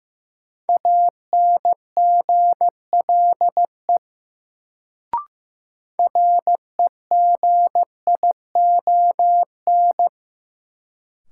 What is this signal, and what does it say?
Morse code, heard as a keyed tone, about 15 wpm, 700 Hz. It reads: ANGLE REGION
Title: Sentences from Top 1500 Words - New Word Drill - 15wpm